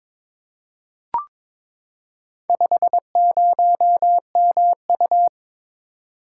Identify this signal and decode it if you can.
Morse code, heard as a keyed tone, about 22 wpm, 700 Hz. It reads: 50MU
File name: practice / 22wpm / Four Letter-Number Combinations - 22wpm